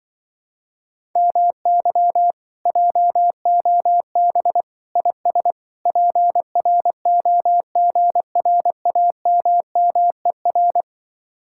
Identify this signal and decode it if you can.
Morse code, heard as a keyed tone, about 25 wpm, 700 Hz. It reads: MY JOB IS PROGRAMMER
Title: Speed-Racing Common QSO Phrases - 20wpm